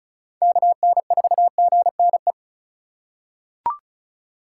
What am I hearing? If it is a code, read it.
Morse code, heard as a keyed tone, about 34 wpm, 700 Hz. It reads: KN4GNE
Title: USA Call Signs - 6 Characters - 2x Character Spacing - 35wpm